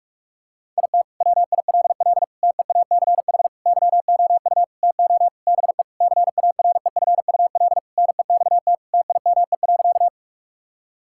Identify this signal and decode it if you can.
Morse code, heard as a keyed tone, about 48 wpm, 700 Hz. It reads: IT WILL TEACH YOU TO BE CAREFUL NEXT TIME.
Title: Sentences from Top 1600 Words 45wpm